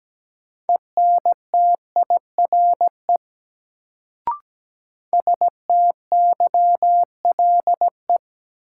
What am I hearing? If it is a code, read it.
Morse code, heard as a keyed tone, about 17 wpm, 700 Hz. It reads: ENTIRE STYLE
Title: Sentences from Top 800 Words - New Word Drill - 17wpm